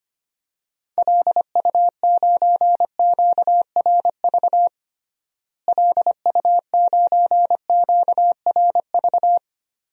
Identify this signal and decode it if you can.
Morse code, heard as a keyed tone, about 25 wpm, 700 Hz. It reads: LU9QRV LU9QRV
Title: DXCC-100 Call Signs - 6 Characters - Repeated 2x - 25wpm